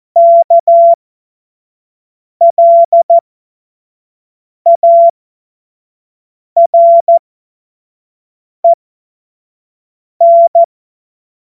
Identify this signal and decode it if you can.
Morse code, heard as a keyed tone, about 14 wpm, spaced as for 5 wpm, 680 Hz. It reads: KLAREN
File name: Morsekurs / Witze / Witz 23. Mai 2023 25 F70